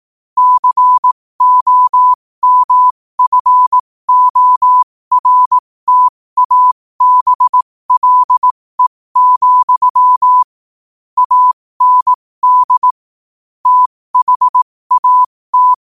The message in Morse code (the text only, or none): COMFORTABLE, AND THAT